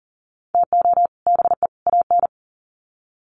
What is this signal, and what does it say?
Morse code, heard as a keyed tone, about 40 wpm, 700 Hz. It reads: TO BE AN
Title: Sets of 3 Words 40wpm